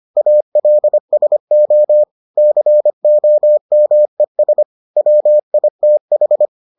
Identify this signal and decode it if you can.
Morse code, heard as a keyed tone, about 25 wpm, 600 Hz. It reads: ALSO COMES WITH